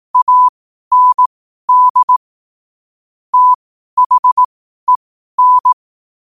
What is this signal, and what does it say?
Morse code, heard as a keyed tone, about 18 wpm, spaced as for 12 wpm, 1000 Hz. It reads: AND THEN
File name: qotdfiles / 12wpm